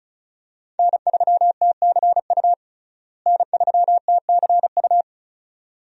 Morse code, heard as a keyed tone, about 35 wpm, 700 Hz. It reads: N3TCU N3TCU
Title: USA Call Signs - 5 Characters - Repeated 2x - 35wpm